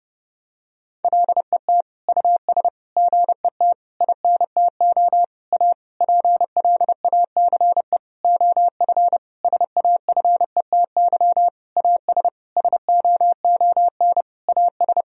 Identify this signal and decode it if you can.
Morse code, heard as a keyed tone, about 32 wpm, 700 Hz. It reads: LET US GET INTO A PLACE OF SAFETY AS SOON AS
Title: Sentences from Top 1000 Words - Review Part 4 - 30wpm